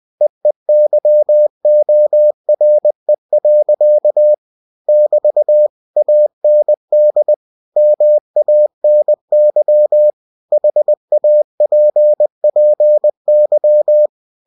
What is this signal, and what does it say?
Morse code, heard as a keyed tone, about 20 wpm, 600 Hz. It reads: EEYORE. <BT> AND MANY HAPPY
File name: cw-books / winnie-the-pooh / winnie-20wpm-page-0047